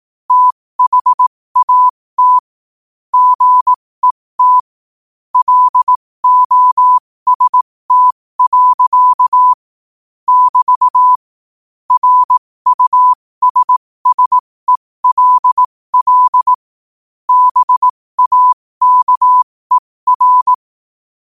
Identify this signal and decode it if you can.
Morse code, heard as a keyed tone, about 18 wpm, spaced as for 15 wpm, 1000 Hz. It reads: THAT GET LOST. <BT> RUSSELL BAKER
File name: qotdfiles / 15wpm